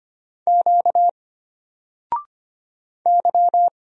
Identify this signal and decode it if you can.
Morse code, heard as a keyed tone, about 25 wpm, 700 Hz. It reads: Q Y
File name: Single-Letter-Number-25wpm